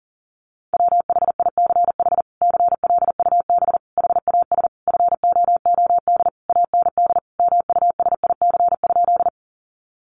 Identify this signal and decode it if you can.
Morse code, heard as a keyed tone, about 40 wpm, 700 Hz. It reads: WHICH CLUB HAS FOOD AND MUSIC?
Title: Sentences from Top 1200 Words 40wpm